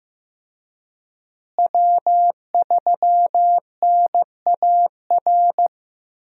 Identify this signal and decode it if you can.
Morse code, heard as a keyed tone, about 15 wpm, 700 Hz. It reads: W3NAR
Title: USA Call Signs - 5 Characters - Repeated 3x - 15wpm